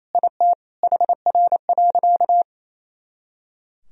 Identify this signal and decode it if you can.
Morse code, heard as a keyed tone, about 28 wpm, 700 Hz. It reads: IT HR.
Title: Sentences from Top 500 Words and Abbreviations 28wpm